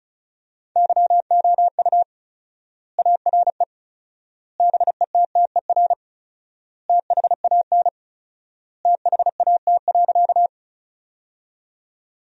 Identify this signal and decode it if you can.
Morse code, heard as a keyed tone, about 34 wpm, 700 Hz. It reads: YOU ARE BETTER THAN THAT.
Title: 4x Word-Spacing - Sentences from Top 400 Words - 35wpm